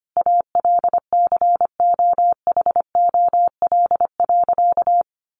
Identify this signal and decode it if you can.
Morse code, heard as a keyed tone, about 25 wpm, 700 Hz. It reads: ALCOHOL.
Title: Sentences from Top 1400 Words 25wpm